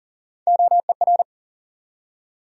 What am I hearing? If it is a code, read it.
Morse code, heard as a keyed tone, about 40 wpm, 700 Hz. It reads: OER
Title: USA Call Sign Postfixes - 3 Letters 40wpm